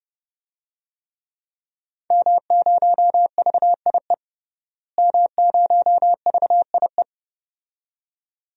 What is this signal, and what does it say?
Morse code, heard as a keyed tone, about 30 wpm, 700 Hz. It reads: M0VIE M0VIE
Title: DXCC-100 Call Signs - 5 Characters - Repeated 2x - 30wpm